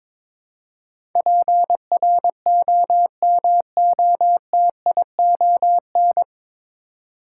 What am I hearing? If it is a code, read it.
Morse code, heard as a keyed tone, about 22 wpm, 700 Hz. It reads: PROMOTION
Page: Morse Code Ninja practice 22wpm